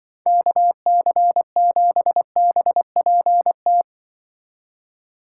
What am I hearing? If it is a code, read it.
Morse code, heard as a keyed tone, about 24 wpm, 700 Hz. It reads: KC7BPT
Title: Speed Racing - USA Call Signs 20wpm